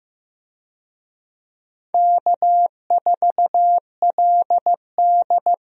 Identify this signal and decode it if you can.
Morse code, heard as a keyed tone, about 15 wpm, 700 Hz. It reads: K4LD
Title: USA Call Signs - 4 Characters - 15wpm